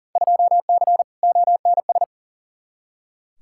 Morse code, heard as a keyed tone, about 40 wpm, 700 Hz. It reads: 2C ONS
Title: Field Day Exchanges Encore - 40wpm